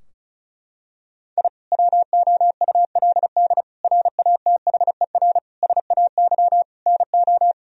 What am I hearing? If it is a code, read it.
Morse code, heard as a keyed tone, about 34 wpm, 700 Hz. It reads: I WOULD RATHER SAY NO
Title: Sentences from Top 1000 Words - Review Part 4 - 35wpm